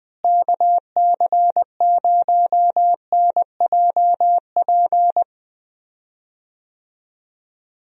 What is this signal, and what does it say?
Morse code, heard as a keyed tone, about 21 wpm, 700 Hz. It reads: KC0NJP
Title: USA Call Signs - 6 Characters - Repeated 2x - 20wpm